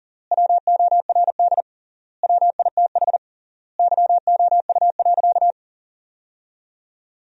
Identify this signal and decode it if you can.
Morse code, heard as a keyed tone, about 40 wpm, 700 Hz. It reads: WORD WITH YOU.
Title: 3x Word-Spacing - Sentences from Top 300 Words - 40wpm